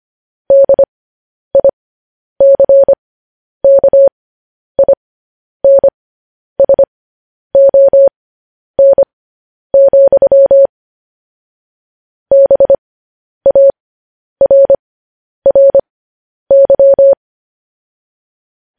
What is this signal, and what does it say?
Morse code, heard as a keyed tone, about 25 wpm, spaced as for 10 wpm, 550 Hz. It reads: DICKINSON, BARRY
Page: news headlines in morse code at 10 wpm